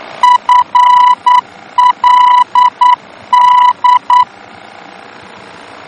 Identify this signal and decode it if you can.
Morse code, heard as a keyed tone, about 9 wpm, 990 Hz. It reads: FLD
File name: germany / mx / FLD-D-VOR